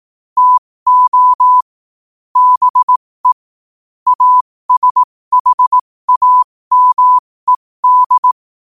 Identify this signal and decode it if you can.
Morse code, heard as a keyed tone, about 18 wpm, spaced as for 15 wpm, 1000 Hz. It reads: TO BE ASHAMED
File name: qotdfiles / 15wpm